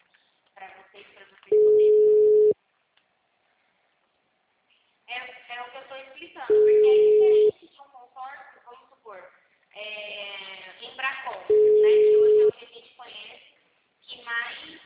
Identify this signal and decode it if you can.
Ringback tone (Europe, 425 Hz), with speech.
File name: out- -2001- - - .324